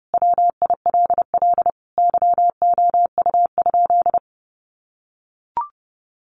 Morse code, heard as a keyed tone, about 30 wpm, 700 Hz. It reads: WILL YOU?
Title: Sentences from Top 1100 Words 30wpm